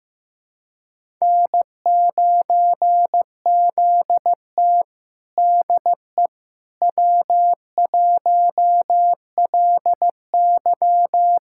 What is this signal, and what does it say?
Morse code, heard as a keyed tone, about 15 wpm, 700 Hz. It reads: N9ZT DE W1LY
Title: USA Callsign DE Callsign - 4 Characters - 15wpm